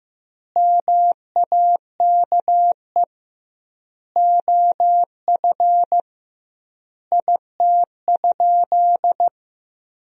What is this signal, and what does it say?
Morse code, heard as a keyed tone, about 16 wpm, 700 Hz. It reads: MAKE OF IT?
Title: 2x Word-Spacing - Sentences from Top 100 Words - 15wpm